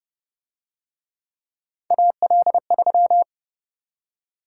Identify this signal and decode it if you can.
Morse code, heard as a keyed tone, about 30 wpm, 700 Hz. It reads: AL3
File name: Speed-Racing - Three Letter-Number Combinations - 25wpm